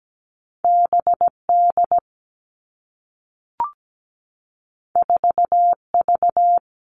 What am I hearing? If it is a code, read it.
Morse code, heard as a keyed tone, about 17 wpm, 700 Hz. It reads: BD 4V
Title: All Cut Numbers - 2-Digit 17wpm